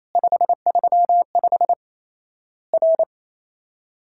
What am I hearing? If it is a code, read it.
Morse code, heard as a keyed tone, about 28 wpm, 700 Hz. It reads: 535 R
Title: Worked All Europe DX Contest - QTC Exchange - 28wpm